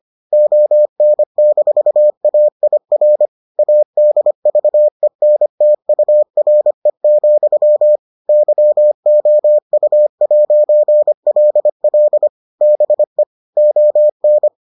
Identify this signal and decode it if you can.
Morse code, heard as a keyed tone, about 25 wpm, 600 Hz. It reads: ON-AIR ADVENTURE, YOU'LL BE ON